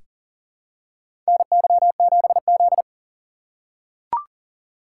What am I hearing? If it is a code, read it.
Morse code, heard as a keyed tone, about 40 wpm, 700 Hz. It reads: NY7Z
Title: K1USN SST QSO Practice - 40wpm